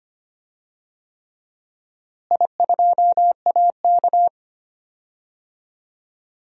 Morse code, heard as a keyed tone, about 25 wpm, 700 Hz. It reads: I2AK
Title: DXCC-100 Call Signs - 4 Characters - Repeated 3x - 25wpm